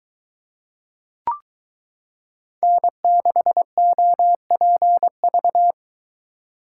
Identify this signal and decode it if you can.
Morse code, heard as a keyed tone, about 22 wpm, 700 Hz. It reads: N6OPV